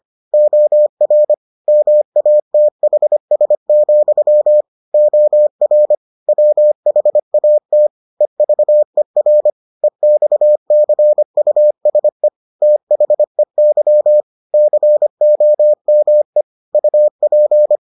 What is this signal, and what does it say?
Morse code, heard as a keyed tone, about 25 wpm, 600 Hz. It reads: OR MATHS, OR WHAT EVER EXCUSE THEY COME UP